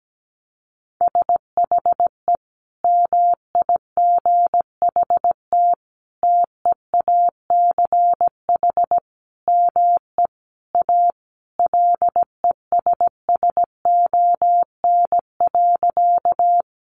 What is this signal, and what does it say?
Morse code, heard as a keyed tone, about 17 wpm, 700 Hz. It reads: SHE MIGHT TEACH ME A LESSON.
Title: Sentences from Top 1200 Words 17wpm